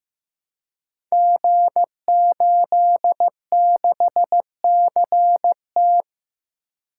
Morse code, heard as a keyed tone, about 16 wpm, 700 Hz. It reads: G86CT
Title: Code Groups - 5 Letter:Numbers - 15wpm